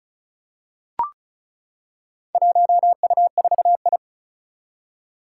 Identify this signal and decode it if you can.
Morse code, heard as a keyed tone, about 34 wpm, 700 Hz. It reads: 1U4I
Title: Four Challenging Letter-Number Combinations 35wpm